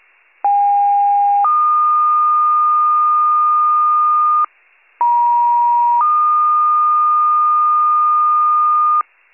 Two-tone page, heard as a one-second tone, then a three-second tone, about 810 then 1220 Hz, about 950 then 1220 Hz, with noise. It is heard from 0.4 to 9.0 s.